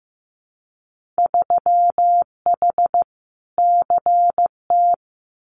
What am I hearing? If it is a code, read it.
Morse code, heard as a keyed tone, about 15 wpm, 700 Hz. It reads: 3H CT
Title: Winter Field Day Exchanges Encore 15wpm